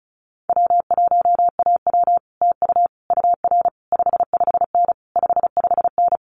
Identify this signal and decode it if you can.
Morse code, heard as a keyed tone, about 36 wpm, 700 Hz. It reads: W1AW TU UR 55N 55N
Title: Common QSO Phrases Encore 35wpm